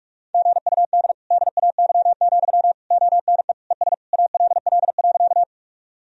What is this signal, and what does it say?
Morse code, heard as a keyed tone, about 46 wpm, 700 Hz. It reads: GUD DAY, ONE ES ALL.
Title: Sentences from Top 200 Words and Abbreviations 45wpm